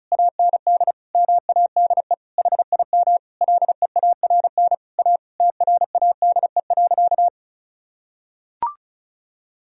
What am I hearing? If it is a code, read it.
Morse code, heard as a keyed tone, about 36 wpm, 700 Hz. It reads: AND MADE HIM LEARN A TRADE.
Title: Sentences from Top 1000 Words - Review Part 4 - 35wpm